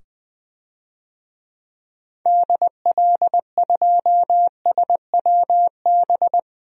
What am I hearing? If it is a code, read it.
Morse code, heard as a keyed tone, about 20 wpm, 700 Hz. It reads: DL2SWB
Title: DXCC-100 Call Signs - 6 Characters - 20wpm